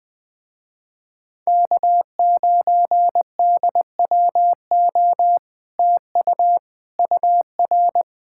Morse code, heard as a keyed tone, about 20 wpm, 700 Hz. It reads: K9DWO TU UR
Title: Speed-Racing Common QSO Phrases Encore - 17wpm